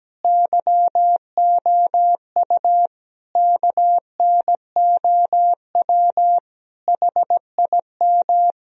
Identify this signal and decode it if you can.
Morse code, heard as a keyed tone, about 17 wpm, 700 Hz. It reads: YOU KNOW HIM
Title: Sentences from Top 1000 Words - Review Part 3 - 17wpm